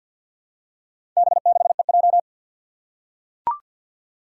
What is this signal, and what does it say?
Morse code, heard as a keyed tone, about 46 wpm, 700 Hz. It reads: B6EJ
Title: Four Challenging Letter-Number Combinations 50wpm